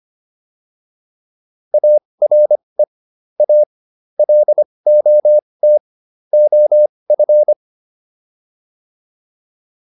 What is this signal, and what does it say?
Morse code, heard as a keyed tone, about 25 wpm, spaced as for 20 wpm, 600 Hz. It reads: ARE A LOT OF